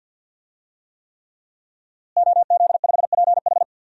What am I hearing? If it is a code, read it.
Morse code, heard as a keyed tone, about 53 wpm, 700 Hz. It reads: OZ5PH
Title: DXCC-100 Call Signs - 5 Characters - Repeated 3x - 50wpm